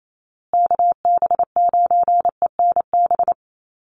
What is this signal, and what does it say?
Morse code, heard as a keyed tone, about 28 wpm, 700 Hz. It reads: KB9ENB
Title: USA Call Signs Repeated 3x 28wpm